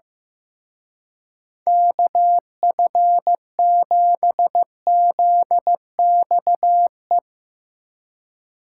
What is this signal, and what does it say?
Morse code, heard as a keyed tone, about 16 wpm, 700 Hz. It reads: KF7ZXE
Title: Speed Racing - USA 2x3 Call Signs 15wpm